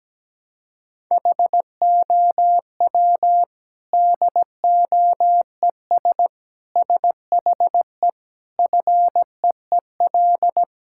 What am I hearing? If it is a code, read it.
Morse code, heard as a keyed tone, about 17 wpm, 700 Hz. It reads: HOW DOES SHE FEEL
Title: Sentences from Top 1000 Words - Review Part 6 - 17wpm